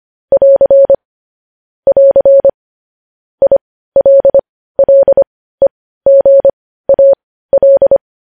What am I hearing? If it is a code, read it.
Morse code, heard as a keyed tone, about 25 wpm, spaced as for 15 wpm, 550 Hz. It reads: <AR> <AR> ILLEGAL
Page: news-headlines-in-morse-code-at-15-wpm